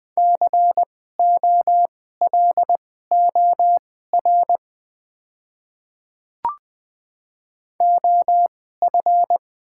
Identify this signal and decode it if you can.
Morse code, heard as a keyed tone, about 20 wpm, spaced as for 14 wpm, 690 Hz. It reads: COLOR OF
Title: 2x Character Spacing - Top 500 Words - 20wpm